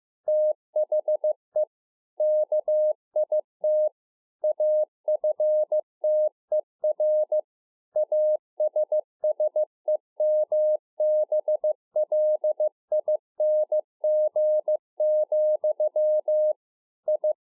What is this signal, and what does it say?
Morse code, heard as a keyed tone, about 15 wpm, 610 Hz. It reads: THE KIT AFTER ASSEMBLING, I
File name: 2011-04-mp3 / A SMALL TRANCEIVER 2 15WPM